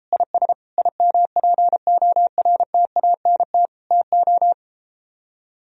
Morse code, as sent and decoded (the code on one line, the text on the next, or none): .. ... / .. -- .--. --- .-. - .- -. - / - ---
IS IMPORTANT TO